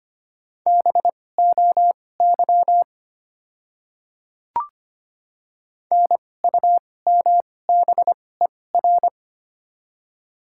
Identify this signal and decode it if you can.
Morse code, heard as a keyed tone, about 25 wpm, spaced as for 18 wpm, 690 Hz. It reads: BOY NUMBER
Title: 2x Character Spacing - Top 300 Words - 25wpm